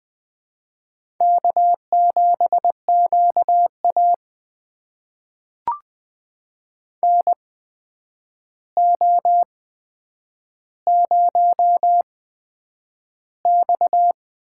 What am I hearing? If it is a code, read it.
Morse code, heard as a keyed tone, about 20 wpm, spaced as for 9 wpm, 700 Hz. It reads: K7QA NO0X
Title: USA Call Signs - 4 Characters - 8x Character Spacing - 20wpm